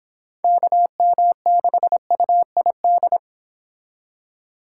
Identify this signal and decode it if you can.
Morse code, heard as a keyed tone, about 26 wpm, 700 Hz. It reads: KM6UID